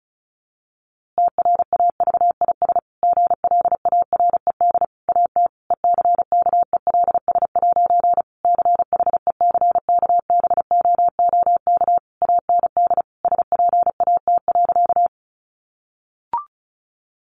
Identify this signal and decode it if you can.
Morse code, heard as a keyed tone, about 34 wpm, 700 Hz. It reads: TRAVIS GLARED AT ECKELS' CHECKBOOK AND SPAT.
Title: A Sound Of Thunder 35wpm